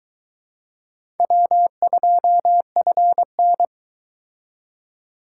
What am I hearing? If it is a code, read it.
Morse code, heard as a keyed tone, about 23 wpm, 700 Hz. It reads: W2FN